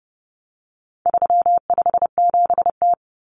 Morse code, heard as a keyed tone, about 30 wpm, 700 Hz. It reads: 357T
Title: Common Cut Numbers - 4-Digit 30wpm